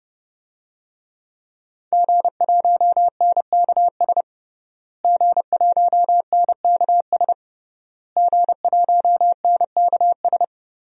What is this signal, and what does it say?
Morse code, heard as a keyed tone, about 30 wpm, 700 Hz. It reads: G1NKS G1NKS G1NKS
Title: DXCC-100 Call Signs - 5 Characters - Repeated 3x - 30wpm